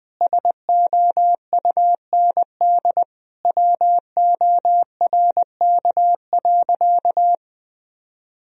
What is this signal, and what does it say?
Morse code, heard as a keyed tone, about 20 wpm, 700 Hz. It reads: SOUND WORK.
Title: Sentences from Top 1000 Words - Review Part 2 - 20wpm